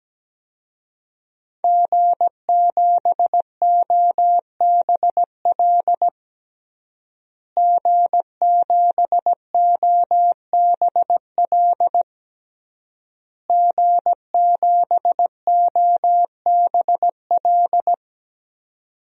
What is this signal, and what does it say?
Morse code, heard as a keyed tone, about 17 wpm, 700 Hz. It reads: G7OBL G7OBL G7OBL
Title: DXCC-100 Call Signs - 5 Characters - Repeated 3x - 17wpm